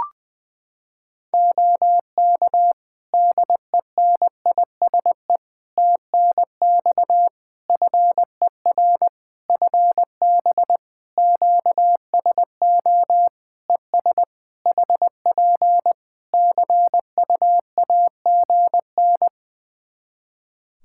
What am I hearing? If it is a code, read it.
Morse code, heard as a keyed tone, about 21 wpm, 700 Hz. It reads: OK DENISE TNX FER FB QSO ES HP CUAGN
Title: Practice QSO - Exchange 4 - Ending - 20wpm